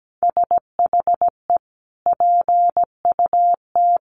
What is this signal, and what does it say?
Morse code, heard as a keyed tone, about 17 wpm, 700 Hz. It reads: SHE PUT
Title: Sentences from Top 1400 Words 17wpm